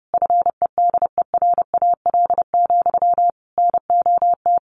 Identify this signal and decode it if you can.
Morse code, heard as a keyed tone, about 30 wpm, 700 Hz. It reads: FEDERAL, NOT
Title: Sentences from Top 1200 Words 30wpm